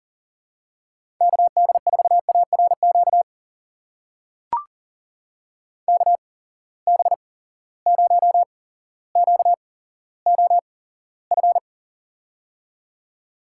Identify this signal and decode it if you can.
Morse code, heard as a keyed tone, about 37 wpm, spaced as for 17 wpm, 700 Hz. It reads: KD4ARQ KB0QOF
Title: USA Call Signs - 6 Characters - 8x Character Spacing - 40wpm